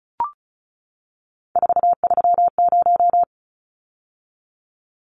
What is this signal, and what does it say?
Morse code, heard as a keyed tone, about 34 wpm, 700 Hz. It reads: 430
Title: Numbers - 3-Digit 35wpm